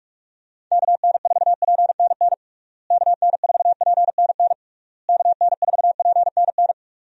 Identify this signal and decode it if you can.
Morse code, heard as a keyed tone, about 46 wpm, 700 Hz. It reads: KN4PNN KN4PNN KN4PNN
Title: USA Call Signs - 6 Characters - Repeated 3x - 45wpm